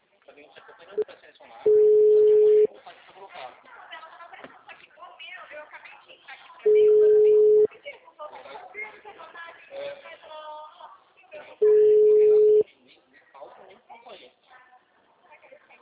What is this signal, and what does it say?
Ringback tone (Europe, 425 Hz), with speech.